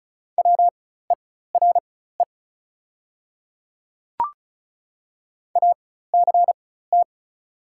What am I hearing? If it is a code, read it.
Morse code, heard as a keyed tone, about 33 wpm, spaced as for 16 wpm, 690 Hz. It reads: WERE ACT
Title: 4x Character Spacing - Top 200 Words - 35wpm